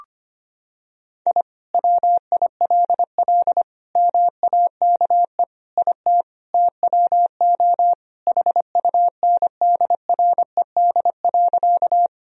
Morse code, heard as a keyed tone, about 25 wpm, 700 Hz. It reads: I WILL MAKE IT TWO HUNDRED.
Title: Sentences from Top 1000 Words - Review Part 2 - 25wpm